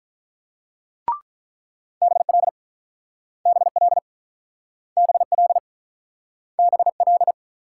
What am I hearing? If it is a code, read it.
Morse code, heard as a keyed tone, about 44 wpm, 700 Hz. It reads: BL BL BL BL
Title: Speed-Racing Multi-letter Phonemes - 35wpm